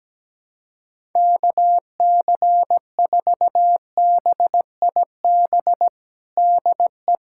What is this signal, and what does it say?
Morse code, heard as a keyed tone, about 17 wpm, 700 Hz. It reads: KC4BIB DE
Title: USA Callsign DE Callsign - 6 Characters - 17wpm